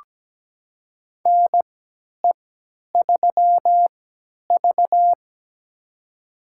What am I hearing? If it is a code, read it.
Morse code, heard as a keyed tone, about 17 wpm, spaced as for 10 wpm, 700 Hz. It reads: NE3V
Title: USA Call Signs - 4 Characters - 3x Character Spacing - 17wpm